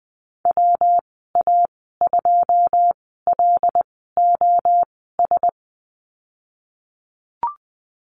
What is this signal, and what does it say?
Morse code, heard as a keyed tone, about 20 wpm, spaced as for 15 wpm, 700 Hz. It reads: WA2LOS
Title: 193 Lesson 31g - Letter v - Call Signs 20wpm